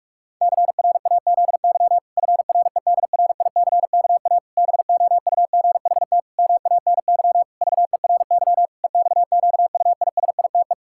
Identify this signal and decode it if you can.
Morse code, heard as a keyed tone, about 46 wpm, 700 Hz. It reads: CRAZY FREDRICKA BOUGHT MANY VERY EXQUISITE
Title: Pangram Challenge v2 - 45wpm